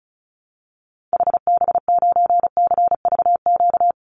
Morse code, heard as a keyed tone, about 34 wpm, 700 Hz. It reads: HB9CVQ
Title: Call Signs Repeated 2x - 35wpm